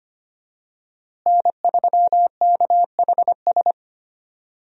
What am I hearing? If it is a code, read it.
Morse code, heard as a keyed tone, about 25 wpm, 700 Hz. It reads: N3KHS
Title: USA Call Signs - 5 Characters - Repeated 3x - 25wpm